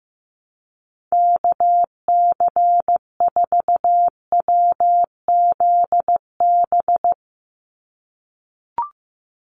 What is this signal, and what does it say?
Morse code, heard as a keyed tone, about 16 wpm, 700 Hz. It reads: KC4WZB
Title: NA SOTA Activator Callsigns 15wpm